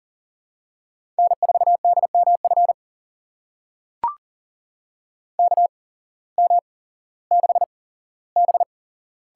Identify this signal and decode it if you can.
Morse code, heard as a keyed tone, about 40 wpm, spaced as for 19 wpm, 700 Hz. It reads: N4DMF KM6B
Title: USA Call Signs - 5 Characters - 8x Character Spacing - 40wpm